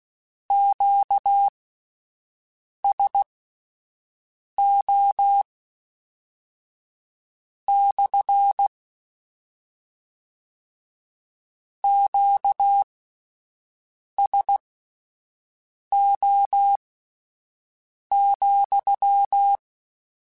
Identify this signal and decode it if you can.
Morse code, heard as a keyed tone, about 16 wpm, spaced as for 5 wpm, 800 Hz. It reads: QSO / QSO,
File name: Q Calls / Q Calls-mp3s / Q CALLS SPSL @10